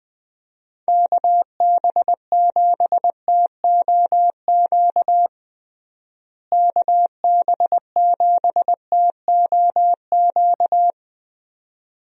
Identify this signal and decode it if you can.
Morse code, heard as a keyed tone, about 21 wpm, 700 Hz. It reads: KB7TOQ KB7TOQ
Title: USA Call Signs - 6 Characters - Repeated 2x - 20wpm